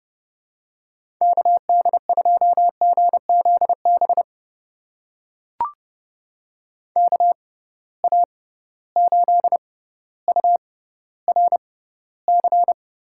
Morse code, heard as a keyed tone, about 31 wpm, spaced as for 16 wpm, 700 Hz. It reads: KD2GZB KA8URC
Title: USA Call Signs - 6 Characters - 6x Character Spacing - 30wpm